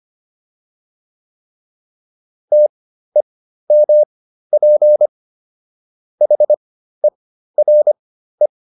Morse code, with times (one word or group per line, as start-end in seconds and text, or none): 2.5-5.1 TEMP
6.2-8.5 HERE